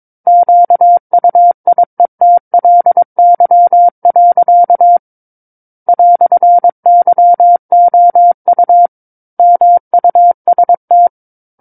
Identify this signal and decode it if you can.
Morse code, heard as a keyed tone, about 22 wpm, 700 Hz. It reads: QUIETLY. "YOU MUST
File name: Book One / 13-Chapter 13 - 22 WPM